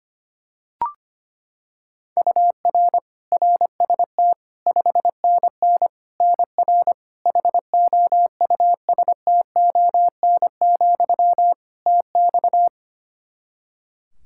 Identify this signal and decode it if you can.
Morse code, heard as a keyed tone, about 25 wpm, 700 Hz. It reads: UR RST 5NN NR HOUSTON, TX
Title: K1USN SST QSO Practice 25wpm